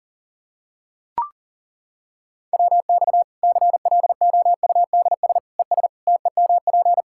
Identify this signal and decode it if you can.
Morse code, heard as a keyed tone, about 40 wpm, 700 Hz. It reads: WX CLOUDS ES TEMP
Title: Practice QSO - Exchange 2 - Info - 40wpm